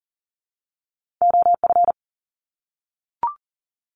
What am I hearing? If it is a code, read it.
Morse code, heard as a keyed tone, about 40 wpm, 700 Hz. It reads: OF
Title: Top 200 Words 40wpm